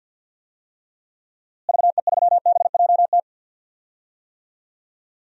Morse code, heard as a keyed tone, about 52 wpm, 700 Hz. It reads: VE3BJT
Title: DXCC-100 Call Signs - 6 Characters - Repeated 3x - 50wpm